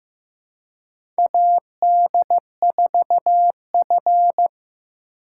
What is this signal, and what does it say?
Morse code, heard as a keyed tone, about 15 wpm, 700 Hz. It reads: AD4F
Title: USA Call Signs - 4 Characters - 2x Character Spacing - 15wpm